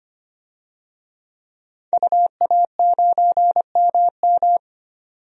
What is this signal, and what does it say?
Morse code, heard as a keyed tone, about 25 wpm, 700 Hz. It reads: UA9MM
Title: DXCC-100 Call Signs - 5 Characters - 25wpm